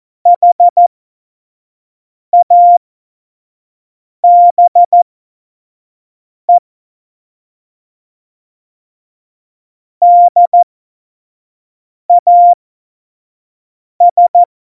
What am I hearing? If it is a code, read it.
Morse code, heard as a keyed tone, about 14 wpm, spaced as for 5 wpm, 700 Hz. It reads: HABE DAS
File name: Morsekurs / Witze / Witz 05. Dezember 2017, 25 F70